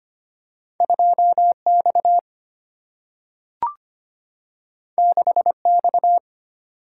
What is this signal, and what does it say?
Morse code, heard as a keyed tone, about 25 wpm, 690 Hz. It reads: 2X 6X
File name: Two Letter-Number Combinations - 25wpm